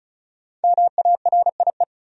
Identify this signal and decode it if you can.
Morse code, heard as a keyed tone, about 36 wpm, 700 Hz. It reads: MARIE
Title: USA Common Names - Max 5 Letters - 35wpm